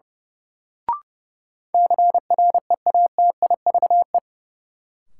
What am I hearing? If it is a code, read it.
Morse code, heard as a keyed tone, about 32 wpm, 700 Hz. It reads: CREATIVE
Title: Sentences from Top 1000 Words - Review Part 5 - New Word Drill - 30wpm